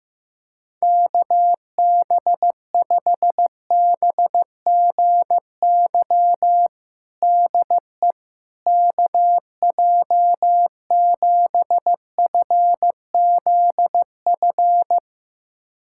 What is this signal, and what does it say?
Morse code, heard as a keyed tone, about 15 wpm, 700 Hz. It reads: KB5BGY DE KJ7FZF
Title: USA Callsign DE Callsign 15wpm